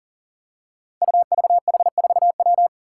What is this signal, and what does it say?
Morse code, heard as a keyed tone, about 40 wpm, 700 Hz. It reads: UVH4W
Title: Five Challenging Letter-Number Combinations 40wpm